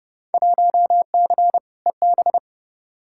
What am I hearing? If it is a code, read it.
Morse code, heard as a keyed tone, about 31 wpm, 700 Hz. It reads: 1C EB
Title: Field Day Exchanges Encore - 30wpm